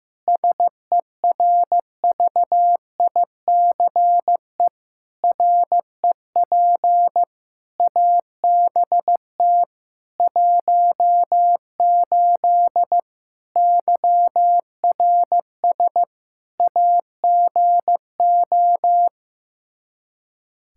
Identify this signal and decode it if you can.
Morse code, heard as a keyed tone, about 15 wpm, 700 Hz. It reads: SERVICE REP ABT 18 YRS AGO
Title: Practice QSO - Exchange 3 - More Info - 15wpm